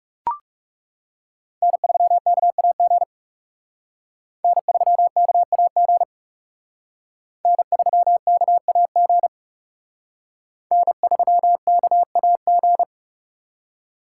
Morse code, heard as a keyed tone, about 37 wpm, 700 Hz. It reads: N3KAG N3KAG N3KAG N3KAG
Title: Speed Racing - USA 1x3 Call Signs 30wpm